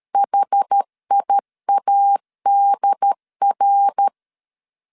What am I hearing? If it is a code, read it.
Morse code, heard as a keyed tone, about 13 wpm, 800 Hz. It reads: HIADR
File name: CW / DET / 1 3 1500C S C 12wpm 2750